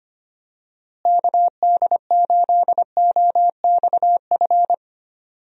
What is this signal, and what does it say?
Morse code, heard as a keyed tone, about 25 wpm, 700 Hz. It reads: KD8OXF
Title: USA Call Signs - 6 Characters - Repeated 3x - 25wpm